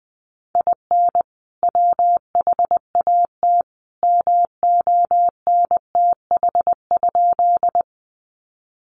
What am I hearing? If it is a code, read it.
Morse code, heard as a keyed tone, about 19 wpm, 700 Hz. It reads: IN WHAT MONTH?
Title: Sentences from Top 800 Words 20wpm